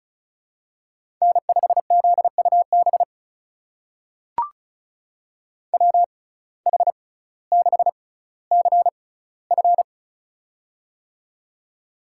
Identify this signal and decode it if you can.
Morse code, heard as a keyed tone, about 36 wpm, spaced as for 18 wpm, 700 Hz. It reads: N5ZUB WH6CF
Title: USA Call Signs - 5 Characters - 6x Character Spacing - 35wpm